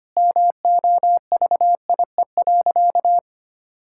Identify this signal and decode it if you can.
Morse code, heard as a keyed tone, about 25 wpm, 700 Hz. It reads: MOVIE.
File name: practice / 25wpm / Sentences with 2-Word Phrases - 25wpm